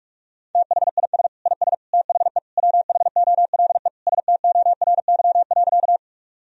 Morse code, heard as a keyed tone, about 43 wpm, 700 Hz. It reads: THIS IS THE WHOLE STORY.
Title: Sentences from Top 1000 Words - Review Part 1 - 45wpm